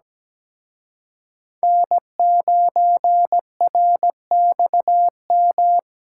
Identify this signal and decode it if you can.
Morse code, heard as a keyed tone, about 20 wpm, 700 Hz. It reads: N9RXM
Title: Speed Racing - USA 1x3 Call Signs 17wpm